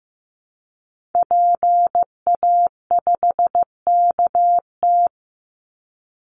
Morse code, heard as a keyed tone, about 15 wpm, 700 Hz. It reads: PA5KT
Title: Call Signs Repeated 3x - 15wpm